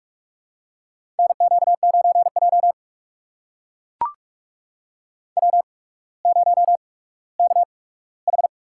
Morse code, heard as a keyed tone, about 44 wpm, spaced as for 19 wpm, 700 Hz. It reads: NQ9J W0KH
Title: USA Call Signs - 4 Characters - 8x Character Spacing - 45wpm